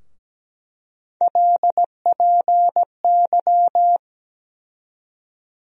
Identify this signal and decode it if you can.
Morse code, heard as a keyed tone, about 17 wpm, 700 Hz. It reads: LPY
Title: USA Call Sign Postfixes - 3 Letters 17wpm